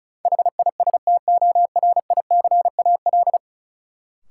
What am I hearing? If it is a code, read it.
Morse code, heard as a keyed tone, about 36 wpm, 700 Hz. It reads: HISTORICAL
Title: Sentences from Top 600 Words - New Word Drill - 35wpm